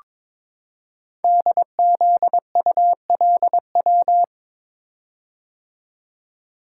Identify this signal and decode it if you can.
Morse code, heard as a keyed tone, about 22 wpm, 700 Hz. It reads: DZULW
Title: Code Groups - 5 Letters - 22wpm